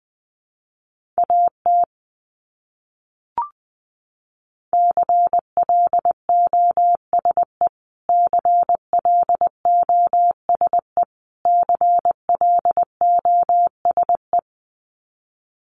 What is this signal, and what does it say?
Morse code, heard as a keyed tone, about 20 wpm, 700 Hz. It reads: AT CLOSE CLOSE CLOSE
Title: Top 500 Words Repeated 20wpm